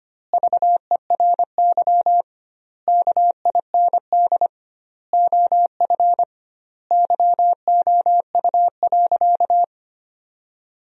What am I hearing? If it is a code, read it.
Morse code, heard as a keyed tone, about 25 wpm, 700 Hz. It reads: VERY KIND OF YOU.
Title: 2x Word-Spacing - Sentences from Top 200 Words - 25wpm